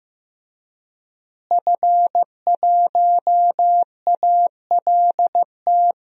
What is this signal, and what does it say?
Morse code, heard as a keyed tone, about 15 wpm, 700 Hz. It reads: F1ALT
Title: DXCC-100 Call Signs - 5 Characters - Repeated 3x - 15wpm